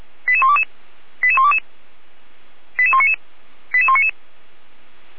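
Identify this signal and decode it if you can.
Five-tone selective call (ZVEI), heard as quick runs of five tones, tones of about 70 ms. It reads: 88122 88122 88199 88199
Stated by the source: ZVEI five-tone selective call received with an SDR.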